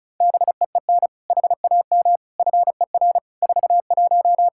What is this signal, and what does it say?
Morse code, heard as a keyed tone, about 36 wpm, 700 Hz. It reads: BEEN HAM FER 41
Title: Practice QSO - Exchange 3 - More Info - 35wpm